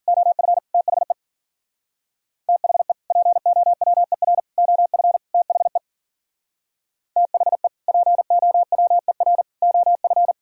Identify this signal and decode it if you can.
Morse code, heard as a keyed tone, about 50 wpm, 700 Hz. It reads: OF THE THE POWER OF THE THE POWER OF